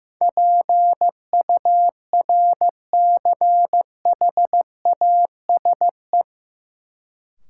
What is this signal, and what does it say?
Morse code, heard as a keyed tone, about 15 wpm, 700 Hz. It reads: PURCHASE
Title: Sentences from Top 800 Words - New Word Drill - 15wpm